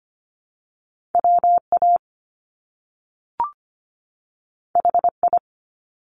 Morse code, heard as a keyed tone, about 24 wpm, 700 Hz. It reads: WA HI
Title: US States 25wpm